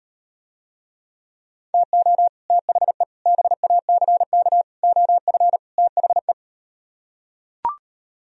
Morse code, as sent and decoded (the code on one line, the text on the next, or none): - --- / - .... / -... .- -.-. -.- / --- ..-. / - .... .
TO THE BACK OF THE